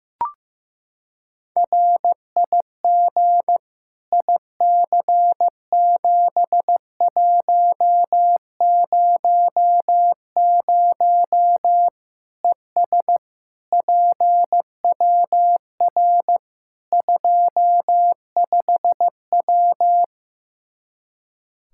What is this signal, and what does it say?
Morse code, heard as a keyed tone, about 15 wpm, 700 Hz. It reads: RIG IC7100 ES PWR 25W
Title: K1USN SST QSO Practice 15wpm